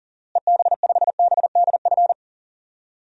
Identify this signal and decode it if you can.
Morse code, heard as a keyed tone, about 40 wpm, 700 Hz. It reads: EB5BDF
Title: DXCC-100 Call Signs - 6 Characters - Repeated 3x - 40wpm